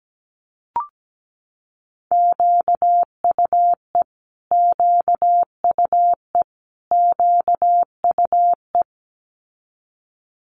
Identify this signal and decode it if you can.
Morse code, heard as a keyed tone, about 17 wpm, 700 Hz. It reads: QUE QUE QUE
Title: Multi-letter Phonemes Repeated 3x 17wpm